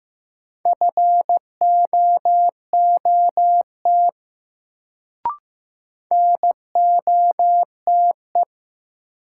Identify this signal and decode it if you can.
Morse code, heard as a keyed tone, about 16 wpm, 700 Hz. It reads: FOOT NOTE
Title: Sets of 5 Words - New Word Drill - 15wpm